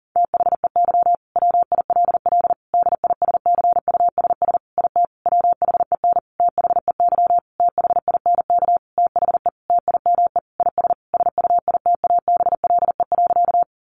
Morse code, heard as a keyed tone, about 40 wpm, 700 Hz. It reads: THEY WILL DISCUSS IT WHEN THEY THINK THE TIME IS SUITABLE.
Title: Sentences from Top 1400 Words 40wpm